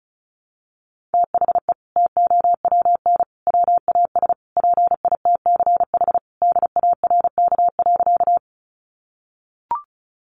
Morse code, heard as a keyed tone, about 34 wpm, 700 Hz. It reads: THE TOWN WAS PITCH DARK.
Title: Sentences from Top 1300 Words 35wpm